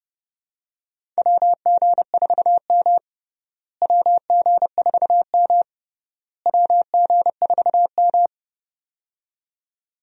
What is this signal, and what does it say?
Morse code, heard as a keyed tone, about 30 wpm, 700 Hz. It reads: WG4M WG4M WG4M
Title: USA Call Signs - 4 Characters - Repeated 3x - 30wpm